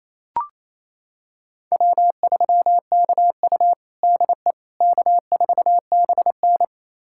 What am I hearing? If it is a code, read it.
Morse code, heard as a keyed tone, about 28 wpm, 700 Hz. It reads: W3KU DE K4BN
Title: USA Callsign DE Callsign - 4 Characters - 28wpm